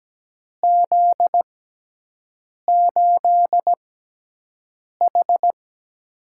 Morse code, heard as a keyed tone, about 17 wpm, spaced as for 6 wpm, 700 Hz. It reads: Z8H
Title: USA Call Signs - 6x Character Spacing - 17wpm